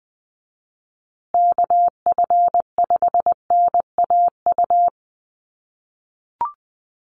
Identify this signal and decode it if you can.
Morse code, heard as a keyed tone, about 20 wpm, 700 Hz. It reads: KF5NAU
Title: USA Call Signs Repeated 2x 20wpm